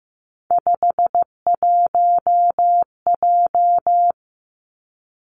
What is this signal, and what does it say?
Morse code, heard as a keyed tone, about 15 wpm, 700 Hz. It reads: 51J
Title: Three Challenging Letter-Number Combinations 15wpm